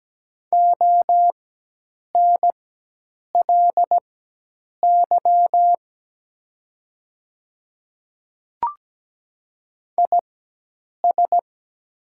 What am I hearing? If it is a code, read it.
Morse code, heard as a keyed tone, about 17 wpm, spaced as for 8 wpm, 700 Hz. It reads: ONLY IS
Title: 4x Character Spacing - Top 200 Words - 17wpm